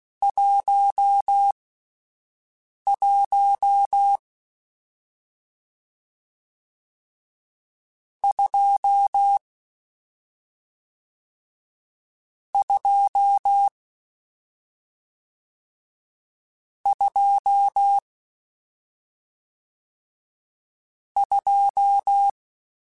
Morse code, heard as a keyed tone, about 16 wpm, spaced as for 5 wpm, 800 Hz. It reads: 11 2 2 2 2
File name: Misc / Misc-mp3s / Numb @5